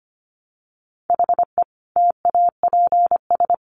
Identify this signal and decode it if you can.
Morse code, heard as a keyed tone, about 25 wpm, 700 Hz. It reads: HE TAPS
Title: Additional Easy Phrases 25wpm